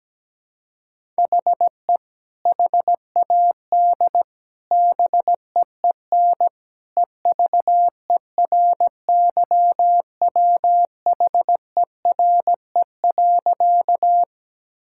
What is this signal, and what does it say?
Morse code, heard as a keyed tone, about 17 wpm, 700 Hz. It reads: HE HAD BEEN EVERYWHERE.
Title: Sentences from Top 1600 Words 17wpm